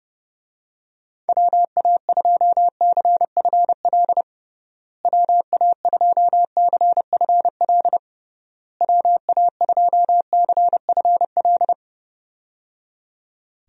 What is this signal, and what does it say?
Morse code, heard as a keyed tone, about 30 wpm, 700 Hz. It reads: WA2CFL WA2CFL WA2CFL
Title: USA Call Signs - 6 Characters - Repeated 3x - 30wpm